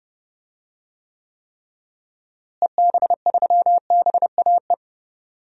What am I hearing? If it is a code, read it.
Morse code, heard as a keyed tone, about 30 wpm, 700 Hz. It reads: EB3BAE
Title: DXCC-100 Call Signs - 6 Characters - Repeated 3x - 30wpm